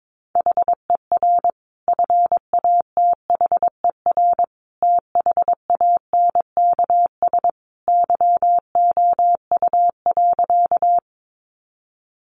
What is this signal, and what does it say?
Morse code, heard as a keyed tone, about 22 wpm, 700 Hz. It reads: HER FATHER THANKS YOU.
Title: Sentences from Top 600 Words 22wpm